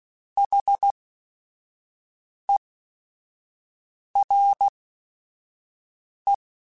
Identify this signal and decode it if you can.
Morse code, heard as a keyed tone, about 16 wpm, spaced as for 5 wpm, 800 Hz. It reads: HERE
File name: Technical Subjects / Technical Subjects-mp3s / AND Gate @5